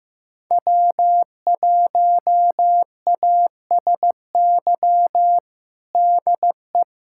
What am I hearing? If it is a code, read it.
Morse code, heard as a keyed tone, about 15 wpm, 700 Hz. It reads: W1ASY DE
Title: USA Callsign DE Callsign - 5 Characters - 15wpm